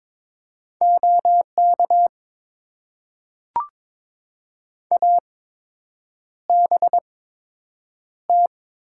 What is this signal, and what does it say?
Morse code, heard as a keyed tone, about 22 wpm, spaced as for 8 wpm, 700 Hz. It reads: OK ABT
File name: Top 90 QSO Elements - 8x Character Spacing - 22wpm